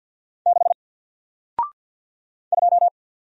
Morse code, heard as a keyed tone, about 48 wpm, 700 Hz. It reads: 6 2
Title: Single Letter-Number Alt 50wpm